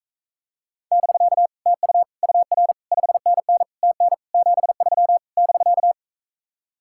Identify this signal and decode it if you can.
Morse code, heard as a keyed tone, about 44 wpm, 700 Hz. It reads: <BK> TU UR 5NN TN 73 <BK>